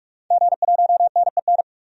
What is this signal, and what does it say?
Morse code, heard as a keyed tone, about 46 wpm, 700 Hz. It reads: G1NEN
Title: DXCC-100 Call Signs - 5 Characters - 45wpm